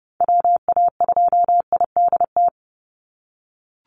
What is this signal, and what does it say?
Morse code, heard as a keyed tone, about 30 wpm, 700 Hz. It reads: WA2IDT
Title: USA Call Signs 30wpm